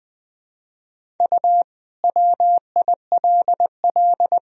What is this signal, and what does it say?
Morse code, heard as a keyed tone, about 21 wpm, 700 Hz. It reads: U WILL
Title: Sentences from Top 400 Words and Abbreviations 20wpm